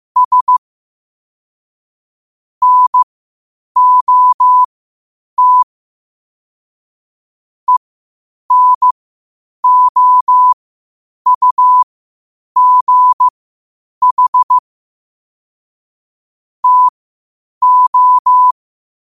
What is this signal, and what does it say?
Morse code, heard as a keyed tone, about 15 wpm, spaced as for 8 wpm, 1000 Hz. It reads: S NOT ENOUGH TO
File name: qotdfiles / 8wpm